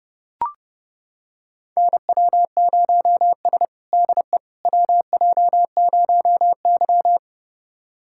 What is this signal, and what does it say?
Morse code, heard as a keyed tone, about 30 wpm, 700 Hz. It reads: NW0S DE WJ0Y
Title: USA Callsign DE Callsign - 4 Characters - 30wpm